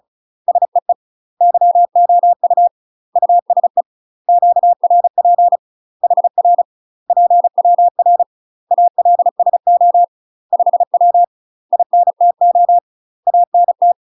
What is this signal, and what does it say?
Morse code, heard as a keyed tone, about 36 wpm, 700 Hz. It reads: SEE YOU USE QRP HR PWR ALSO 5W INTO ANT
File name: UK MASTER QSOs TRAINING / 35WPM / UK MASTER QSOs 0038